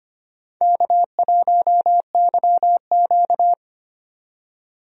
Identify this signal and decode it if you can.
Morse code, heard as a keyed tone, about 25 wpm, 700 Hz. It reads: K1YQ
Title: USA Call Signs - 4 Characters - 25wpm